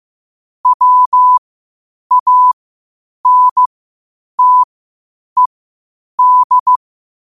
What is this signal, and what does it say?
Morse code, heard as a keyed tone, about 15 wpm, spaced as for 8 wpm, 1000 Hz. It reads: WANTED
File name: qotdfiles / 8wpm